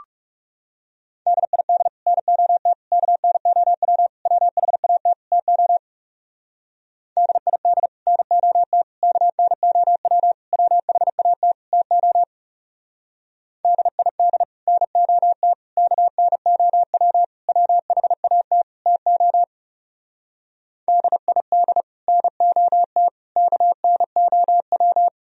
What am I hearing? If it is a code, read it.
Morse code, heard as a keyed tone, about 35 wpm, 700 Hz. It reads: DID NOT KNOW WHAT TO DID NOT KNOW WHAT TO DID NOT KNOW WHAT TO DID NOT KNOW
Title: Speed-Racing - Sets of 5 Words 30wpm